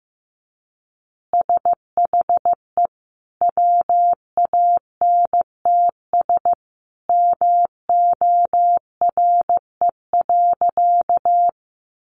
Morse code, heard as a keyed tone, about 15 wpm, 700 Hz. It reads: SHE WANTS MORE.
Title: Easy Phrases 15wpm